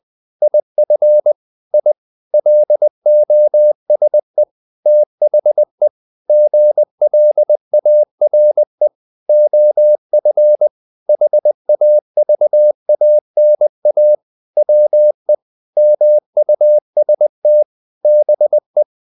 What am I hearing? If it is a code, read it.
Morse code, heard as a keyed tone, about 20 wpm, 600 Hz. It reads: IF I LOSE THE GLARE OF HAVANA WE MUST BE
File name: the old man and the sea / the old man and the sea part 1 20wpm